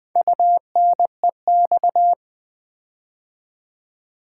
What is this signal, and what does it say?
Morse code, heard as a keyed tone, about 20 wpm, 700 Hz. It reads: UNEX
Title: Build A Word - 4 Letters At A Time - 10-Letter Words - 20wpm